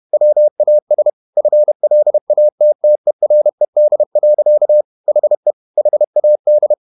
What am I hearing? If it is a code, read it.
Morse code, heard as a keyed tone, about 31 wpm, 600 Hz. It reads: WAS FLATTERED. HE HAD
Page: Never 31wpm | CW med Gnister